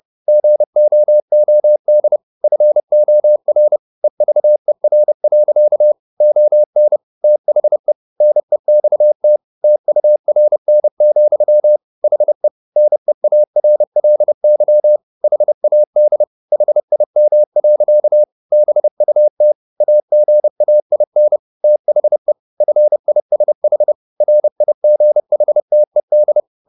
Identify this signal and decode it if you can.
Morse code, heard as a keyed tone, about 30 wpm, 600 Hz. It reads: GOOD FOR EVER. ON THE NEXT TURN, HE NEARLY HAD HIM. BUT AGAIN THE FISH RIGHTED